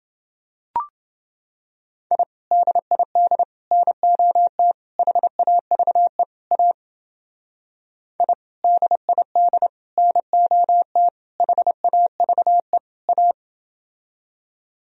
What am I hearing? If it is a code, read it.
Morse code, heard as a keyed tone, about 30 wpm, 700 Hz. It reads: I DID NOT HAVE A I DID NOT HAVE A